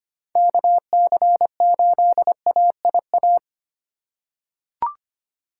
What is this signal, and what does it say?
Morse code, heard as a keyed tone, about 25 wpm, 700 Hz. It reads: KC8AIA
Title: USA Call Signs - 6 Characters - 3x Character Spacing - 25wpm